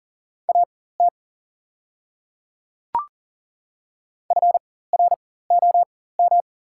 Morse code, heard as a keyed tone, about 40 wpm, spaced as for 19 wpm, 700 Hz. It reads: AT FROM
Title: 4x Character Spacing - Top 200 Words - 40wpm